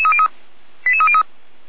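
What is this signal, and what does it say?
Five-tone selective call (ZVEI), heard as quick runs of five tones, tones of about 70 ms. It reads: 88303
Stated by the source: ZVEI five-tone selective call received by SDR, Feuerwehr Orlamünde (f85360).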